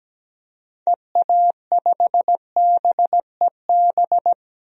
Morse code, heard as a keyed tone, about 17 wpm, 700 Hz. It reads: EA5BEB
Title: DXCC-100 Call Signs - 6 Characters - 17wpm